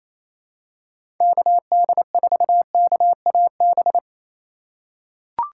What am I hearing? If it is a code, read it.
Morse code, heard as a keyed tone, about 28 wpm, 700 Hz. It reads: KD4KAB
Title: USA Call Signs - 6 Characters - 3x Character Spacing - 28wpm